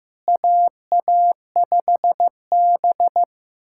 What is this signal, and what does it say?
Morse code, heard as a keyed tone, about 15 wpm, 700 Hz. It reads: AA5B
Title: K1USN SST QSO Practice - 15wpm